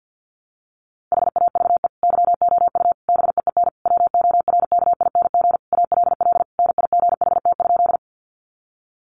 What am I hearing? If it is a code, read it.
Morse code, heard as a keyed tone, about 48 wpm, 700 Hz. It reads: HAVE YOU BEEN WORKING ALL NIGHT?
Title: Sentences from Top 700 Words 50wpm